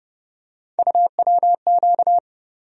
Morse code, heard as a keyed tone, about 30 wpm, 700 Hz. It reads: UWQ
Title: USA Call Sign Postfixes - 3 Letters 30wpm